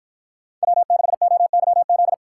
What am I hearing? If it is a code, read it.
Morse code, heard as a keyed tone, about 52 wpm, 700 Hz. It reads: W6OYZ